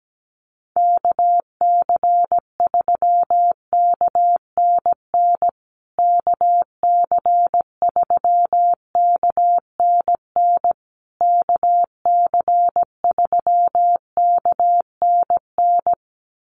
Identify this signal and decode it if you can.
Morse code, heard as a keyed tone, about 17 wpm, 700 Hz. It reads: KC3KNN KC3KNN KC3KNN
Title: USA Call Signs Repeated 3x 17wpm